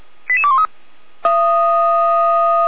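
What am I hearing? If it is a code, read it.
Five-tone selective call (ZVEI), heard as quick runs of five tones, tones of about 70 ms. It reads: 88213
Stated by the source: ZVEI five-tone call received by SDR, from audio.f85360.feuerwehr-orlamuende.org fire brigade server.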